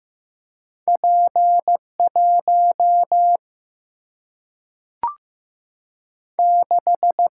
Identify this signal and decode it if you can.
Morse code, heard as a keyed tone, about 16 wpm, 700 Hz. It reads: P 1 6
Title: Single-Letter Single-Number 15wpm